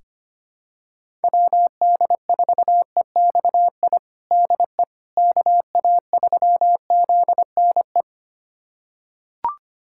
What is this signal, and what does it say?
Morse code, heard as a keyed tone, about 25 wpm, 700 Hz. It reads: WD4EXI DE KA3ZNE
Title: USA Callsign DE Callsign - 6 Characters - 25wpm